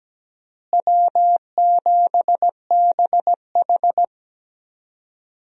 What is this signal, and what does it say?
Morse code, heard as a keyed tone, about 17 wpm, 700 Hz. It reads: W7BH
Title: USA Call Signs - 4 Characters - Repeated 2x - 17wpm